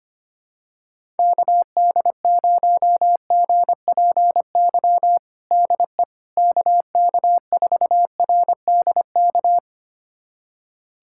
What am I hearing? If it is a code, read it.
Morse code, heard as a keyed tone, about 25 wpm, 700 Hz. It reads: KD0GPY DE KK4RDK
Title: USA Callsign DE Callsign 25wpm